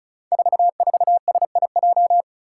Morse code, heard as a keyed tone, about 36 wpm, 700 Hz. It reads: 44SIJ
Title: Five Challenging Letter-Number Combinations 35wpm